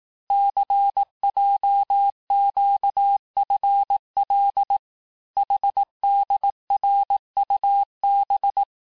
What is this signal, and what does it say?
Morse code, heard as a keyed tone, about 18 wpm, 800 Hz. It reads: CJQFL HDRUB
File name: cours-du-lundi files / cours-1588-texte-chiffre-vitesse-17MPM